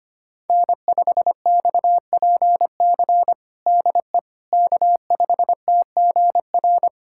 Morse code, heard as a keyed tone, about 25 wpm, 700 Hz. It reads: N5XPC DE K5TGR
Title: USA Callsign DE Callsign - 5 Characters - 25wpm